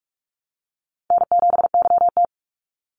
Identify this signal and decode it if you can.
Morse code, heard as a keyed tone, about 44 wpm, 700 Hz. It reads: N7YT
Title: Easy Call Signs Repeated 2x - 45wpm